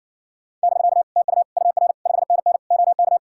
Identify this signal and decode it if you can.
Morse code, heard as a keyed tone, about 67 wpm, 700 Hz. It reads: <BK> TU UR 5NN OK